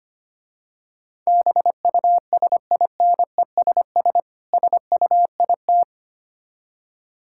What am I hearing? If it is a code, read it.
Morse code, heard as a keyed tone, about 26 wpm, 700 Hz. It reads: BUSINESS SUIT
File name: practice / 25wpm / Difficult Words - Speed-Racing with Hints - 25wpm